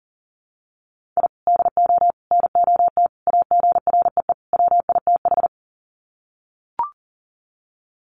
Morse code, heard as a keyed tone, about 39 wpm, 700 Hz. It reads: I DO NOT AGREE WITH
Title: Sets of 5 Words Encore 40wpm